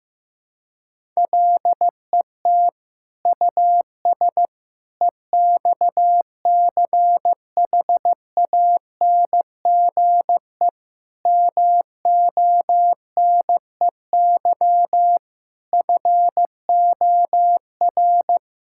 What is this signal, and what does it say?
Morse code, heard as a keyed tone, about 15 wpm, 700 Hz. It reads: LET US EXCHANGE MONEY FOR
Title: Sentences from Top 1000 Words - Review Part 6 - 15wpm